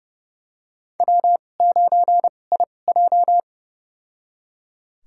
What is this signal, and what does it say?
Morse code, heard as a keyed tone, about 30 wpm, spaced as for 22 wpm, 700 Hz. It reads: W9IJ
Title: USA Call Signs - 2x Character Spacing - 30wpm